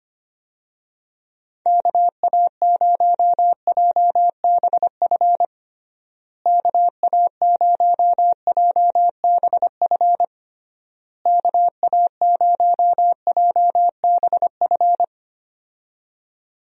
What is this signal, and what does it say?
Morse code, heard as a keyed tone, about 25 wpm, 700 Hz. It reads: KA0JBF KA0JBF KA0JBF
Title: USA Call Signs - 6 Characters - Repeated 3x - 25wpm